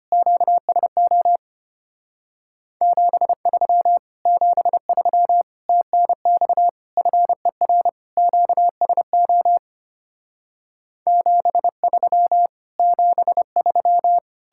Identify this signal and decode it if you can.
Morse code, heard as a keyed tone, about 32 wpm, 700 Hz. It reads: QSO 73 73 TNX FER QSO 73 73